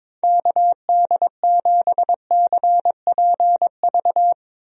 Morse code, heard as a keyed tone, about 21 wpm, 700 Hz. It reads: KD7CPV
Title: USA Call Signs - 6 Characters - Repeated 2x - 22wpm